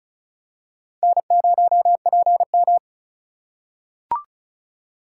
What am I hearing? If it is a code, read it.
Morse code, heard as a keyed tone, about 36 wpm, 700 Hz. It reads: N0PM
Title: USA Call Signs - 4 Characters - 4x Character Spacing - 35wpm